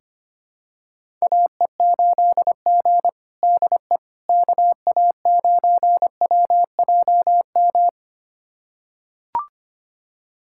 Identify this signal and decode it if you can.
Morse code, heard as a keyed tone, about 25 wpm, 700 Hz. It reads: AE8G DE KA9WJM
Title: USA Callsign DE Callsign 25wpm